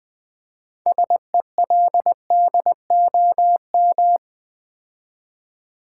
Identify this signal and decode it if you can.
Morse code, heard as a keyed tone, about 20 wpm, 700 Hz. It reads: SELDOM
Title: Build A Word - 3 Letters At A Time - 6-Letter Words - 20wpm